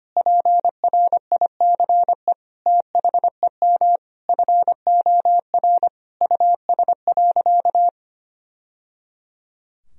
Morse code, heard as a keyed tone, about 25 wpm, 700 Hz. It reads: PRICE THEM FOR US.
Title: Sentences from Top 1000 Words - Review Part 4 - 25wpm